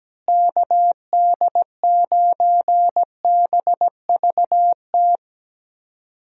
Morse code, heard as a keyed tone, about 17 wpm, 700 Hz. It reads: KD9BVT
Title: USA Call Signs - 6 Characters - Repeated 2x - 17wpm